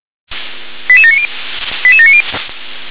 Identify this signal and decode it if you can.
Five-tone selective call (ZVEI), heard as quick runs of five tones, tones of about 70 ms. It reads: 88790 88790